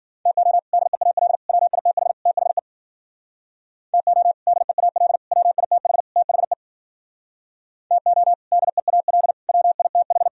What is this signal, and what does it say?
Morse code, heard as a keyed tone, about 51 wpm, 700 Hz. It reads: TO DEAL WITH THE TO DEAL WITH THE TO DEAL WITH